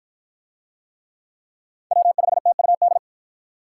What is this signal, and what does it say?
Morse code, heard as a keyed tone, about 46 wpm, 700 Hz. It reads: W5TUD